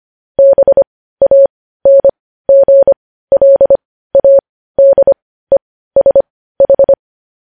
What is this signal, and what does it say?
Morse code, heard as a keyed tone, about 25 wpm, spaced as for 15 wpm, 550 Hz. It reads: BANGLADESH
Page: News Headlines in Morse Code at 15 WPM